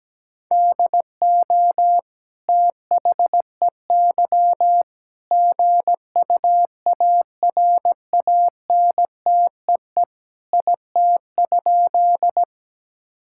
Morse code, heard as a keyed tone, about 17 wpm, 700 Hz. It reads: DO THEY GUARANTEE IT?
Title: Sentences from Top 1600 Words 17wpm